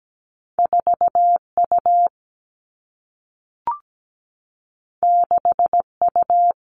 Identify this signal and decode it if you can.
Morse code, heard as a keyed tone, about 17 wpm, 700 Hz. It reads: 4 U 6 U
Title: Two Challenging Letter-Number Combinations 17wpm